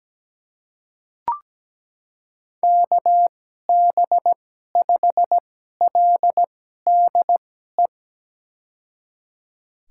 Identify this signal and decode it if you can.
Morse code, heard as a keyed tone, about 17 wpm, spaced as for 12 wpm, 700 Hz. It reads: KB5LDE
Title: USA Call Signs - 6 Characters - 2x Character Spacing - 17wpm